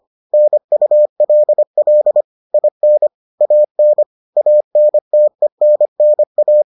Morse code, heard as a keyed tone, about 25 wpm, 600 Hz. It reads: NULL IN AN ANTENNA